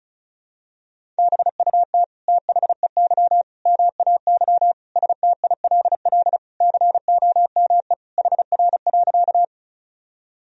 Morse code, heard as a keyed tone, about 36 wpm, 700 Hz. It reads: BUT THEY MAY STILL COME HR.
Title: Sentences from Top 300 Words and Abbreviations 35wpm